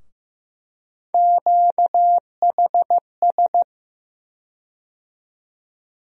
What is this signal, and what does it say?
Morse code, heard as a keyed tone, about 15 wpm, 700 Hz. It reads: QHS
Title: USA Call Sign Postfixes - 3 Letters 15wpm